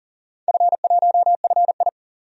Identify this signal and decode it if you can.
Morse code, heard as a keyed tone, about 42 wpm, 700 Hz. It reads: F1FI
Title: DXCC-100 Call Signs - 4 Characters - 40wpm